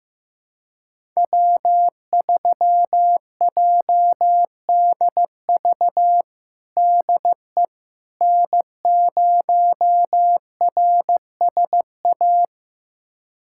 Morse code, heard as a keyed tone, about 15 wpm, 700 Hz. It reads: W3JDV DE N0RSA
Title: USA Callsign DE Callsign - 5 Characters - 15wpm